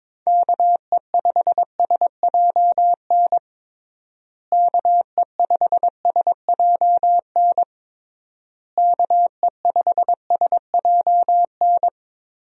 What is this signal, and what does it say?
Morse code, heard as a keyed tone, about 22 wpm, 700 Hz. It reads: KE5SJN KE5SJN KE5SJN
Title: USA Call Signs - 6 Characters - Repeated 3x - 22wpm